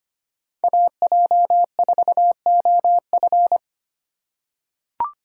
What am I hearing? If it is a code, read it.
Morse code, heard as a keyed tone, about 25 wpm, 700 Hz. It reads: AJ4OF
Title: USA Call Signs - 5 Characters - 2x Character Spacing - 25wpm